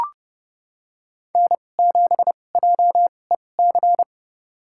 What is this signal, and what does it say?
Morse code, heard as a keyed tone, about 32 wpm, spaced as for 22 wpm, 700 Hz. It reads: N7JEC
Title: USA Call Signs - 5 Characters - 2x Character Spacing - 30wpm